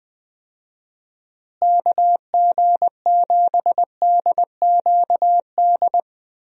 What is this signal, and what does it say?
Morse code, heard as a keyed tone, about 20 wpm, 700 Hz. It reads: KG7DQD
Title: USA Call Signs - 6 Characters - Repeated 2x - 20wpm